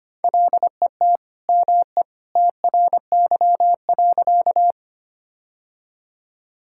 Morse code, heard as a keyed tone, about 25 wpm, 700 Hz. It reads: LET ME TRY.
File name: practice / 25wpm / Sentences with 2-Word Phrases - 25wpm